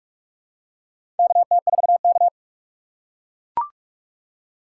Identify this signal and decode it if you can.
Morse code, heard as a keyed tone, about 43 wpm, 700 Hz. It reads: KT4K
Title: USA Call Signs - 4 Characters - 6x Character Spacing - 45wpm